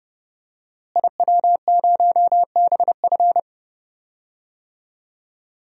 Morse code, heard as a keyed tone, about 30 wpm, 700 Hz. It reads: IW0BF
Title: DXCC-100 Call Signs - 5 Characters - Repeated 2x - 30wpm